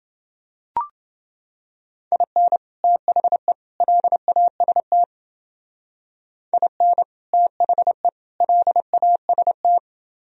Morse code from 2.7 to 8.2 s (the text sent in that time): THE LAST IN THE